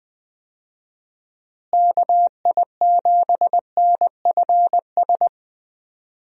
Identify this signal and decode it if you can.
Morse code, heard as a keyed tone, about 21 wpm, 700 Hz. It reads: KI7NFS
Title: USA Call Signs - 6 Characters - Repeated 2x - 20wpm